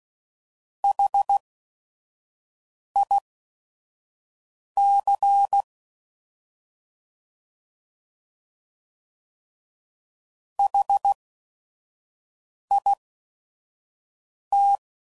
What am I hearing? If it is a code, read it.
Morse code, heard as a keyed tone, about 16 wpm, spaced as for 5 wpm, 800 Hz. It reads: HIC HIT